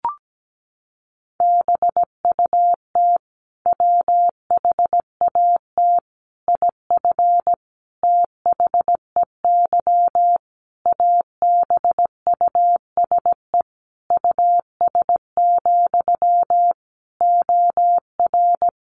Morse code, heard as a keyed tone, about 17 wpm, 700 Hz. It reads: BUT WHAT IF THEY ABUSE US, OR
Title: Sentences from Top 1400 Words 17wpm